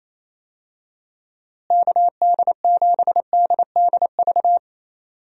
Morse code, heard as a keyed tone, about 28 wpm, 700 Hz. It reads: KD7DDV
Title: USA Call Signs - 6 Characters - Repeated 2x - 28wpm